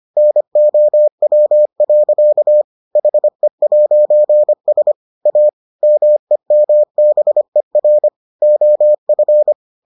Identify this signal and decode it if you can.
Morse code, heard as a keyed tone, about 25 wpm, 600 Hz. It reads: NOW. HE'S A MEMBER OF